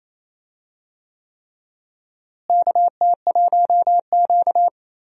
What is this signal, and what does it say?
Morse code, heard as a keyed tone, about 27 wpm, 700 Hz. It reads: KT1Q
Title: USA Call Signs - 4 Characters - Repeated 3x - 28wpm